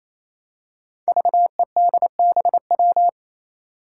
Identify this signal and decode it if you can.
Morse code, heard as a keyed tone, about 28 wpm, 700 Hz. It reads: VEDBW
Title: Five Challenging Letter-Number Combinations 28wpm